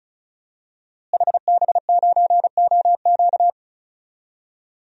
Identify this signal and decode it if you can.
Morse code, heard as a keyed tone, about 36 wpm, 700 Hz. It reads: HB9OQ
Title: DXCC-100 Call Signs - 5 Characters - Repeated 3x - 35wpm